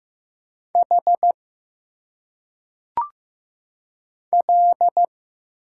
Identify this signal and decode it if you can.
Morse code, heard as a keyed tone, about 15 wpm, 700 Hz. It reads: H L
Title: Single Letters - Phonetic Pronunciation 15wpm